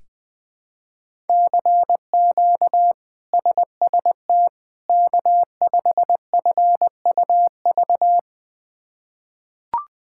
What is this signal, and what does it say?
Morse code, heard as a keyed tone, about 21 wpm, 700 Hz. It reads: CQ SST K5FUV
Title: K1USN SST QSO Practice - 20wpm